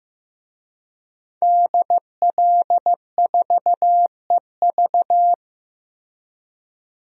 Morse code, heard as a keyed tone, about 15 wpm, 700 Hz. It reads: DL4EV
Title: DXCC-100 Call Signs - 5 Characters - Repeated 3x - 15wpm